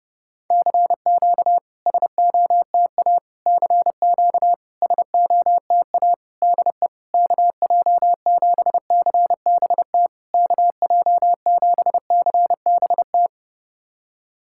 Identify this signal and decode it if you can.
Morse code, heard as a keyed tone, about 30 wpm, 700 Hz. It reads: CQ SOTA CQ SOTA DE KJ7CBT KJ7CBT
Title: SOTA QSO Practice 30wpm